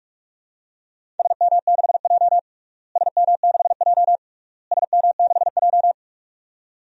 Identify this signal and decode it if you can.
Morse code, heard as a keyed tone, about 46 wpm, 700 Hz. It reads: SM6J SM6J SM6J
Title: DXCC-100 Call Signs - 4 Characters - Repeated 3x - 45wpm